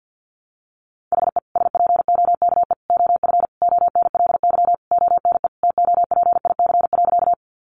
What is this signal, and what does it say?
Morse code, heard as a keyed tone, about 50 wpm, 700 Hz. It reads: HE SPOKE OF ONLY ONE TOPIC.
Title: Sentences from Top 1200 Words 50wpm